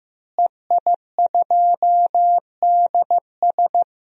Morse code, heard as a keyed tone, about 16 wpm, 700 Hz. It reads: EI2DS
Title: DXCC-100 Call Signs - 5 Characters - Repeated 3x - 15wpm